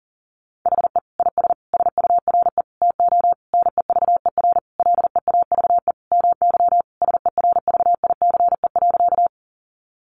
Morse code, heard as a keyed tone, about 40 wpm, 700 Hz. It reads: HE IS SURE TO NEVER LEAVE MY SERVICE.
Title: Sentences from Top 700 Words 40wpm